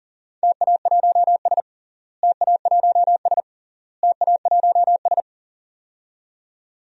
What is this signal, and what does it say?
Morse code, heard as a keyed tone, about 40 wpm, 700 Hz. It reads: TA1S TA1S TA1S
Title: DXCC-100 Call Signs - 4 Characters - Repeated 3x - 40wpm